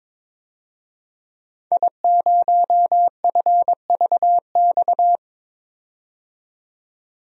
Morse code, heard as a keyed tone, about 22 wpm, 700 Hz. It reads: I0FVX
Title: DXCC-100 Call Signs - 5 Characters - 22wpm